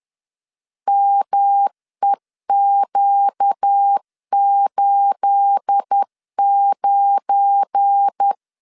Morse code, heard as a keyed tone, about 10 wpm, 800 Hz. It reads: MEQ89
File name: CW / DET / 1 9 1500C S B 10wpm 4439